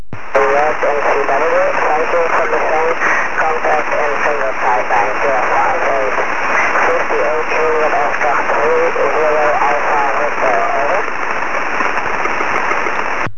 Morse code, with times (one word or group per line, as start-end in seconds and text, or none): none